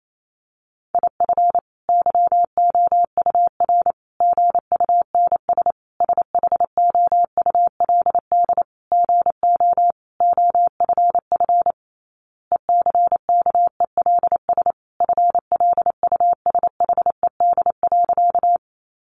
Morse code, heard as a keyed tone, about 28 wpm, 700 Hz. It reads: IF YOUR GUNS SHOULD GO OFF ECKELS FLUSHED.
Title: A Sound Of Thunder 28wpm